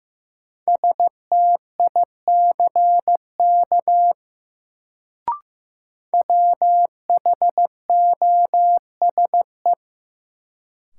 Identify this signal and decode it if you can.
Morse code, heard as a keyed tone, about 15 wpm, 700 Hz. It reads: STICK WHOSE
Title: Sentences from Top 1200 Words - New Word Drill - 15wpm